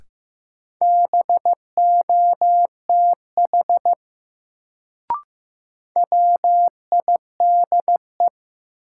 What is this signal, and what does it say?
Morse code, heard as a keyed tone, about 15 wpm, 700 Hz. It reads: BOTH WIDE
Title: Sets of 4 Words - New Word Drill - 15wpm